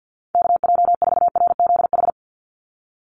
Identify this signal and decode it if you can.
Morse code, heard as a keyed tone, about 50 wpm, 700 Hz. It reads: KJ4RZH
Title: USA Call Signs 50wpm